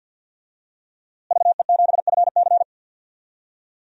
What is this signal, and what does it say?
Morse code, heard as a keyed tone, about 51 wpm, 700 Hz. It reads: VE7FC
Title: DXCC-100 Call Signs - 5 Characters - Repeated 2x - 50wpm